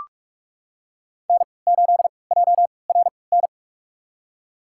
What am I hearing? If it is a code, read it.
Morse code, heard as a keyed tone, about 44 wpm, spaced as for 25 wpm, 700 Hz. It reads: N8JRN
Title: USA Call Signs - 5 Characters - 3x Character Spacing - 45wpm